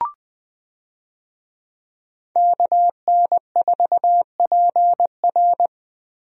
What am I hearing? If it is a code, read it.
Morse code, heard as a keyed tone, about 20 wpm, 700 Hz. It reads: KN4PR
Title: USA Call Signs - 5 Characters - Repeated 2x - 20wpm